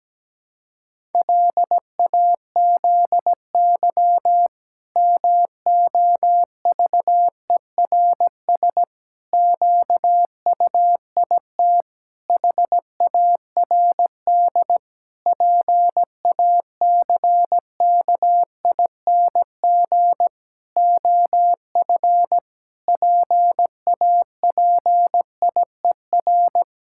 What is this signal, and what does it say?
Morse code, heard as a keyed tone, about 17 wpm, 700 Hz. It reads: LAZY MOVERS QUIT HARD PACKING OF PAPIER
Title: Pangram Challenge v2 - 17wpm